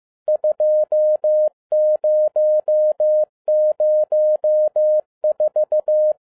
Morse code, heard as a keyed tone, about 15 wpm, 610 Hz. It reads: 2004
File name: 2012-01-mp3 / n17-SUMMARY OF FEA NET IN 2011-15wpm